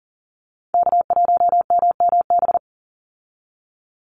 Morse code, heard as a keyed tone, about 40 wpm, 700 Hz. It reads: K1MMB
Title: Call Signs Repeated 2x - 40wpm